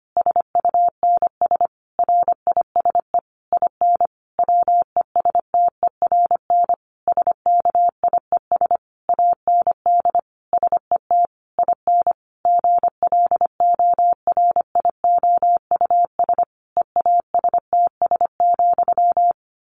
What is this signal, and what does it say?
Morse code, heard as a keyed tone, about 25 wpm, 700 Hz. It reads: SUNS RISE IN WESTERN SKIES AND SET IN GLORIOUS EASTS,
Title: A Sound Of Thunder 25wpm